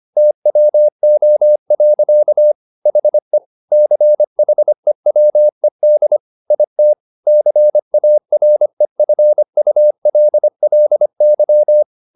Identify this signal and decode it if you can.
Morse code, heard as a keyed tone, about 25 wpm, 600 Hz. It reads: TWO. HE CHEWED IT CAREFULLY
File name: the old man and the sea part 1 25wpm